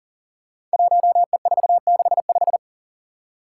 Morse code, heard as a keyed tone, about 42 wpm, 700 Hz. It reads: 1E465
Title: Code Groups - 5 Letter:Numbers - 40wpm